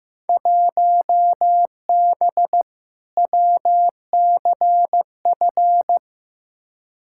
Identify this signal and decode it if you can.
Morse code, heard as a keyed tone, about 15 wpm, 700 Hz. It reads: 1B WCF
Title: Field Day Exchanges Encore - 15wpm